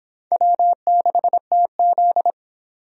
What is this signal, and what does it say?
Morse code, heard as a keyed tone, about 22 wpm, 700 Hz. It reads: W6TZ